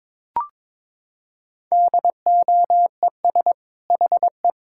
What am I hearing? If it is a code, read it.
Morse code, heard as a keyed tone, about 22 wpm, 700 Hz. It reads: DOES HE
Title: Sentences from Top 1000 Words - Review Part 5 - 22wpm